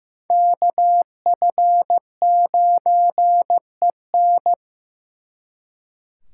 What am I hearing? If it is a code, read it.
Morse code, heard as a keyed tone, about 18 wpm, 700 Hz. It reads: KF9EN
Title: Speed Racing - USA Call Signs 15wpm